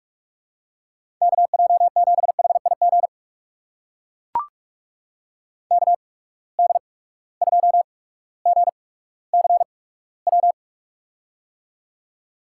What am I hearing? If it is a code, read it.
Morse code, heard as a keyed tone, about 44 wpm, spaced as for 19 wpm, 700 Hz. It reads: KJ7HIG KD2GCW
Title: USA Call Signs - 6 Characters - 8x Character Spacing - 45wpm